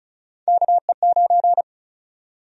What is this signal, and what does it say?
Morse code, heard as a keyed tone, about 36 wpm, 700 Hz. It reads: KE9
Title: USA Call Sign Prefixes - Repeated 2x - 35wpm